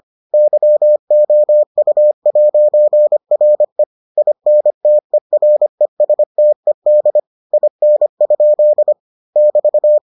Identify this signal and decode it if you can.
Morse code, heard as a keyed tone, about 25 wpm, 600 Hz. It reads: YOU'RE INTERESTED IN? <BT>